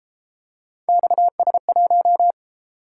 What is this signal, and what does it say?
Morse code, heard as a keyed tone, about 32 wpm, 700 Hz. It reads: XS1